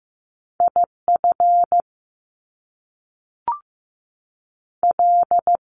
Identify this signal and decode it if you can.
Morse code, heard as a keyed tone, about 15 wpm, 700 Hz. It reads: IF L
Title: Top Two Letter Combinations in Words 15wpm